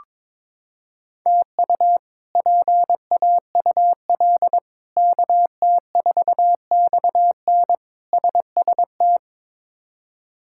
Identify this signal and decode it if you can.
Morse code, heard as a keyed tone, about 22 wpm, 700 Hz. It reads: TU PAUL KT4XN SST
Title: K1USN SST QSO Practice - 22wpm